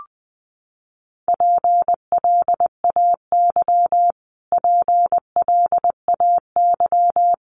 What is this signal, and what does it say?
Morse code, heard as a keyed tone, about 20 wpm, 700 Hz. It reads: PLAY PLAY
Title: Top 300 Words Repeated 20wpm